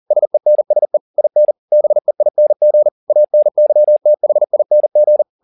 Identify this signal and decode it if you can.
Morse code, heard as a keyed tone, about 40 wpm, 600 Hz. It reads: SENSE IN BEING ANYTHING